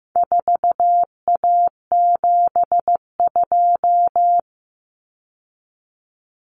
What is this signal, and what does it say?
Morse code, heard as a keyed tone, about 15 wpm, 700 Hz. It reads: 4A72
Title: Common Cut Numbers - 4-Digit 15wpm